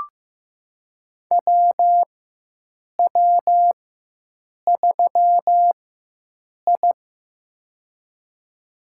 Morse code, heard as a keyed tone, about 16 wpm, spaced as for 7 wpm, 700 Hz. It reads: WW3I
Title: USA Call Signs - 4 Characters - 4x Character Spacing - 15wpm